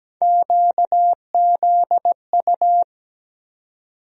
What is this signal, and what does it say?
Morse code, heard as a keyed tone, about 17 wpm, 700 Hz. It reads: QZU
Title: USA Call Sign Postfixes - 3 Letters 17wpm